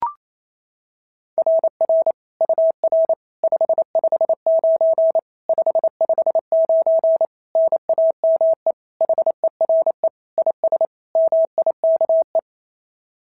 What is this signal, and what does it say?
Morse code, heard as a keyed tone, about 28 wpm, 690 Hz. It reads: RR UR 559 559 NAME HERE IS MIKE
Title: SOTA QSO Practice 28wpm